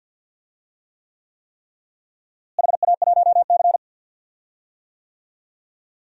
Morse code, heard as a keyed tone, about 51 wpm, 700 Hz. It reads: HA1C
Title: DXCC-100 Call Signs - 4 Characters - Repeated 3x - 50wpm